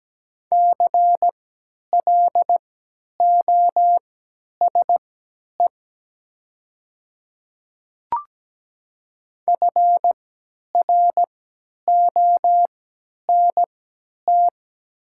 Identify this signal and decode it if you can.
Morse code, heard as a keyed tone, about 17 wpm, spaced as for 10 wpm, 700 Hz. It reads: CLOSE FRONT
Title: 3x Character Spacing - Top 500 Words - 17wpm